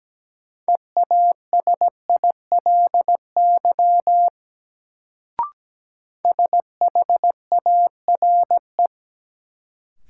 Morse code, heard as a keyed tone, about 17 wpm, 700 Hz. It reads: EASILY SHARE
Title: Sentences from Top 900 Words - New Word Drill - 17wpm